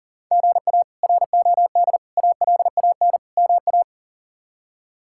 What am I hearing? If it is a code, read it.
Morse code, heard as a keyed tone, about 40 wpm, 700 Hz. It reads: GA ROD ALAN MA
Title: K1USN SST QSO Practice - 40wpm